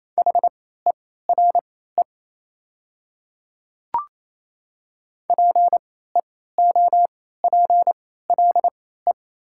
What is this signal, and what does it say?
Morse code, heard as a keyed tone, about 28 wpm, spaced as for 16 wpm, 700 Hz. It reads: HERE PEOPLE
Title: 3x Character Spacing - Top 200 Words - 28wpm